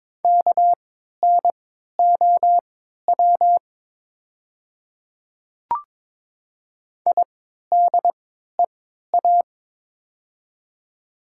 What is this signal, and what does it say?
Morse code, heard as a keyed tone, about 22 wpm, spaced as for 12 wpm, 700 Hz. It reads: KNOW IDEA
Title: 3x Character Spacing - Top 300 Words - 22wpm